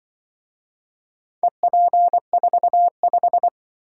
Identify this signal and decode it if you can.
Morse code, heard as a keyed tone, about 24 wpm, 700 Hz. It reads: EP45
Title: Speed-Racing - Four Letter-Number Combinations - 20wpm